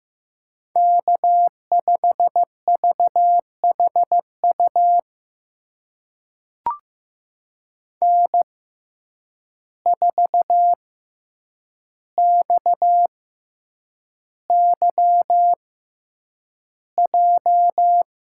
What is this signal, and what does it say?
Morse code, heard as a keyed tone, about 15 wpm, spaced as for 8 wpm, 700 Hz. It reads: K5VHU N4XYJ
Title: USA Call Signs - 5 Characters - 6x Character Spacing - 15wpm